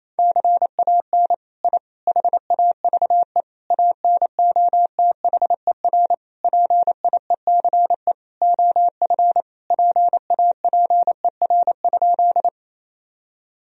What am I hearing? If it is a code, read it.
Morse code, heard as a keyed tone, about 27 wpm, 700 Hz. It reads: CAN I HAVE ANOTHER PIECE OF PAPER?
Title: Sentences from Top 1000 Words - Review Part 2 - 28wpm